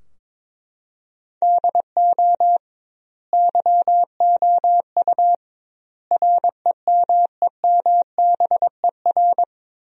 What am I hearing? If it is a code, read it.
Morse code, heard as a keyed tone, about 21 wpm, 700 Hz. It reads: DO YOU REMEMBER
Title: 2x Word-Spacing - Sentences from Top 400 Words - 22wpm